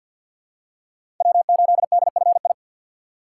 Morse code, heard as a keyed tone, about 47 wpm, 690 Hz. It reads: W8DFI
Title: USA Call Signs - 5 Characters - 3x Character Spacing - 50wpm